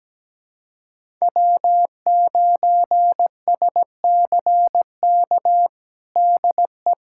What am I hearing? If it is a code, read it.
Morse code, heard as a keyed tone, about 17 wpm, 700 Hz. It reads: W9SCK DE
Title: USA Callsign DE Callsign - 5 Characters - 17wpm